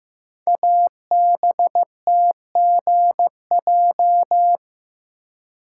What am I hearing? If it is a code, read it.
Morse code, heard as a keyed tone, about 15 wpm, 700 Hz. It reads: ABTGJ
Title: Code Groups - 5 Letters - 15wpm